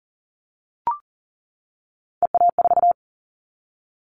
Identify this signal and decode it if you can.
Morse code, heard as a keyed tone, about 39 wpm, 700 Hz. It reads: EA4
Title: Callsign Prefixes for Top 25 DXCC - 40wpm